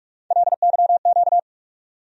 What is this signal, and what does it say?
Morse code, heard as a keyed tone, about 46 wpm, 700 Hz. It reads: LYQ
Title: USA Call Sign Postfixes - 3 Letters 45wpm